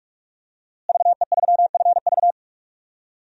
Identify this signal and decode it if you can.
Morse code, heard as a keyed tone, about 46 wpm, 700 Hz. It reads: VE3FV
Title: DXCC-100 Call Signs - 5 Characters - Repeated 3x - 45wpm